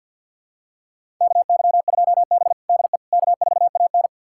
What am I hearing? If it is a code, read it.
Morse code, heard as a keyed tone, about 51 wpm, 700 Hz. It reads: KY2B DE K4AN
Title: USA Callsign DE Callsign - 4 Characters - 50wpm